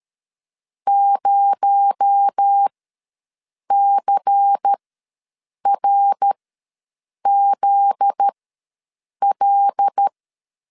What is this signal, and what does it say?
Morse code, heard as a keyed tone, about 13 wpm, spaced as for 7 wpm, 800 Hz. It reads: 0CRZL
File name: LEZIONI / 7 GF9Z / 7 GF9Z ws 12wpm 3321